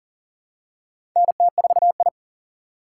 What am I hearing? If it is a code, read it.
Morse code, heard as a keyed tone, about 40 wpm, 700 Hz. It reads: NT4I
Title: USA Call Signs - 4 Characters - 4x Character Spacing - 40wpm